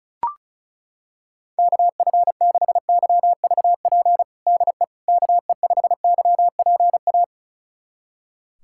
Morse code, heard as a keyed tone, about 34 wpm, 700 Hz. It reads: KF6YVP DE KE5YPA
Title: USA Callsign DE Callsign 35wpm